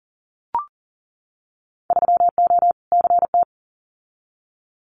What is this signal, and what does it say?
Morse code, heard as a keyed tone, about 40 wpm, 700 Hz. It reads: 3O CT
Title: Winter Field Day Exchanges 40wpm